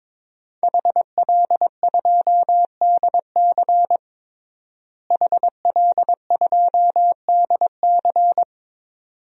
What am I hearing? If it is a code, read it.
Morse code, heard as a keyed tone, about 22 wpm, 700 Hz. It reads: HL2DC HL2DC
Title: DXCC-100 Call Signs - 5 Characters - Repeated 2x - 22wpm